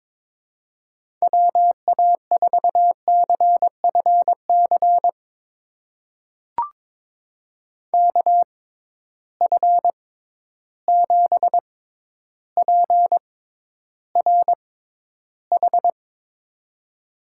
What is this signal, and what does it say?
Morse code, heard as a keyed tone, about 22 wpm, spaced as for 11 wpm, 700 Hz. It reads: WA4CFC KF7PRH
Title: USA Call Signs - 6 Characters - 6x Character Spacing - 22wpm